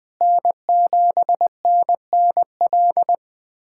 Morse code, heard as a keyed tone, about 20 wpm, 700 Hz. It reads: N7NNL
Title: USA Callsign DE Callsign - 5 Characters - 20wpm